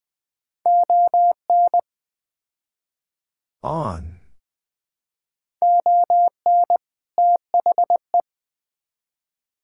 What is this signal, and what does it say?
Morse code, heard as a keyed tone, about 21 wpm, 700 Hz. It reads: ON ON THE
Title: Build A Phrase - 4-Word Phrases - 20wpm